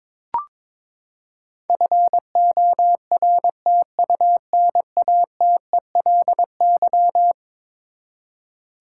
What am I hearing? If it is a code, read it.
Morse code, heard as a keyed tone, about 22 wpm, 700 Hz. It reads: FORTUNATELY
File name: practice / 22wpm / Eleven Letter Words - 22wpm